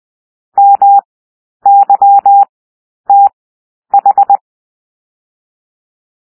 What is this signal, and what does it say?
Morse code, heard as a keyed tone, about 20 wpm, 800 Hz. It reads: M Y T H
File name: cwmp3 / W1 SOUND WORDS 20 wpm